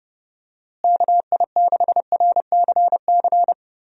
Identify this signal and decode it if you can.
Morse code, heard as a keyed tone, about 31 wpm, 700 Hz. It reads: KI6RCC
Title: USA Call Signs - 6 Characters - Repeated 2x - 30wpm